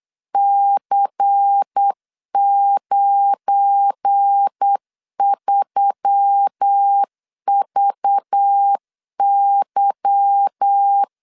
Morse code, heard as a keyed tone, about 8 wpm, 800 Hz. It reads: C93VY
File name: LEZIONI / 10 SIGNS / 10C DOT 8wpm 3649